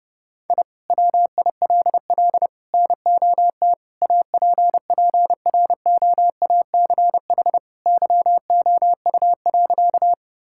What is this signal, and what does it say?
Morse code, heard as a keyed tone, about 30 wpm, 700 Hz. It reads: I WILL NOT APPROACH YOU.
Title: Sentences from Top 1000 Words - Review Part 6 - 30wpm